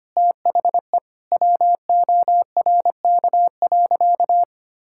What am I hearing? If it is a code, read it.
Morse code, heard as a keyed tone, about 25 wpm, 700 Hz. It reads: THE WORK.
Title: Sentences from Top 1000 Words - Review Part 6 - 25wpm